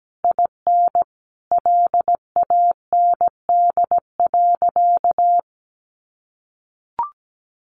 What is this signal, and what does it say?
Morse code, heard as a keyed tone, about 17 wpm, 700 Hz. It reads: IN LAND.
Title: Sentences from Top 1100 Words 17wpm